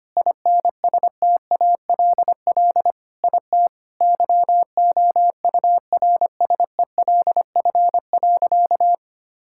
Morse code, heard as a keyed tone, about 25 wpm, 700 Hz. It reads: INSTALL IT YOURSELF.
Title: Sentences from Top 1600 Words 25wpm